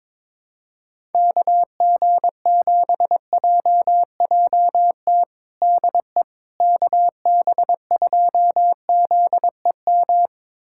Morse code, heard as a keyed tone, about 22 wpm, 700 Hz. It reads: KG7JJT DE KB2ZEM
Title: USA Callsign DE Callsign - 6 Characters - 22wpm